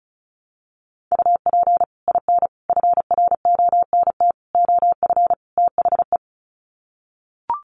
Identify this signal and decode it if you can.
Morse code, heard as a keyed tone, about 34 wpm, 700 Hz. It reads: UP IN FRONT OF THE
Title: Sets of 5 Words Encore2 35wpm